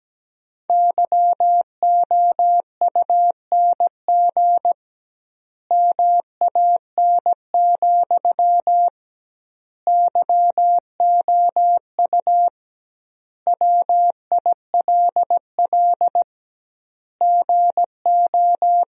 Morse code, heard as a keyed tone, about 17 wpm, 700 Hz. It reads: YOUNG MAN, YOU WILL GO
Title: 2x Word-Spacing - Sentences from Top 400 Words - 17wpm